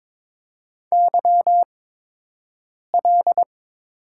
Morse code, heard as a keyed tone, about 22 wpm, spaced as for 6 wpm, 700 Hz. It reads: YL
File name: Top 90 QSO Elements - 8x Character Spacing - 22wpm